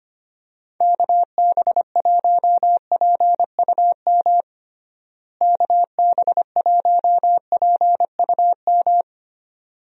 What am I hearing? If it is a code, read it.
Morse code, heard as a keyed tone, about 25 wpm, 700 Hz. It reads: KB1PUM KB1PUM
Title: USA Call Signs - 6 Characters - Repeated 2x - 25wpm